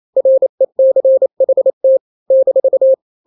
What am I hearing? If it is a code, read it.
Morse code, heard as a keyed tone, about 28 wpm, 530 Hz. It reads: RECHT -
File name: daily / 27wpm